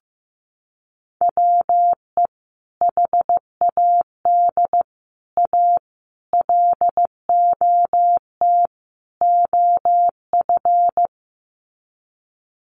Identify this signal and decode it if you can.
Morse code, heard as a keyed tone, about 15 wpm, 700 Hz. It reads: WE HAD A LOT OF
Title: Sets of 5 Words Encore 15wpm